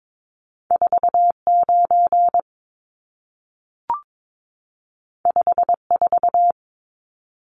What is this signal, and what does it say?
Morse code, heard as a keyed tone, about 22 wpm, 700 Hz. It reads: 49 54
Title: Numbers - 2-Digit 22wpm